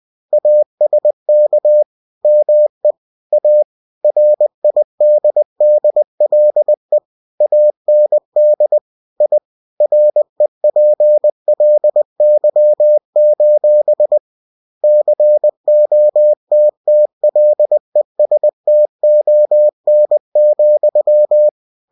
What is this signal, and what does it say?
Morse code, heard as a keyed tone, about 20 wpm, 600 Hz. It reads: ASK ME A RIDDLE AND I REPLY: COTTLESTON,
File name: cw-books / winnie-the-pooh / winnie-20wpm-page-0046